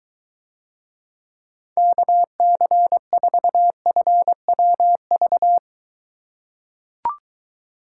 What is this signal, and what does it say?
Morse code, heard as a keyed tone, about 21 wpm, 700 Hz. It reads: KC4FWV